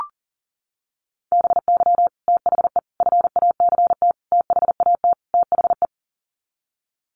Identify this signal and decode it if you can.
Morse code, heard as a keyed tone, about 40 wpm, 700 Hz. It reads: BY THE FACT THAT THE
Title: Sets of 5 Words Encore 40wpm